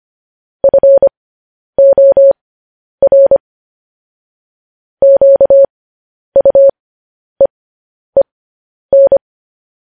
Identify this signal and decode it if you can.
Morse code, heard as a keyed tone, about 25 wpm, spaced as for 10 wpm, 550 Hz. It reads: FOR QUEEN
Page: news-headlines-in-morse-code-at-10-wpm